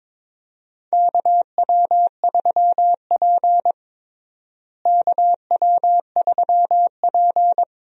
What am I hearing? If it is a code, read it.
Morse code, heard as a keyed tone, about 23 wpm, 700 Hz. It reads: KW3P KW3P
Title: USA Call Signs - 4 Characters - Repeated 3x - 22wpm